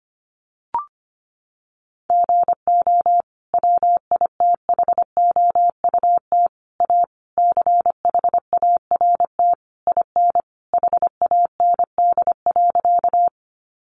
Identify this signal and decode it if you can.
Morse code, heard as a keyed tone, about 25 wpm, 700 Hz. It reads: GO WITHOUT A CHART IN HAND.
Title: Sentences from Top 1300 Words 25wpm